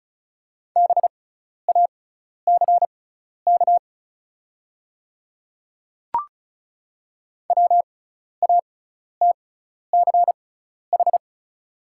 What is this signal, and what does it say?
Morse code, heard as a keyed tone, about 36 wpm, spaced as for 12 wpm, 690 Hz. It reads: BACK WATCH
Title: 6x Character Spacing - Top 500 Words - 35wpm